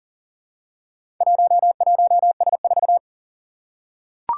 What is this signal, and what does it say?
Morse code, heard as a keyed tone, about 42 wpm, 700 Hz. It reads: 11S4
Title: Four Challenging Letter-Number Combinations 40wpm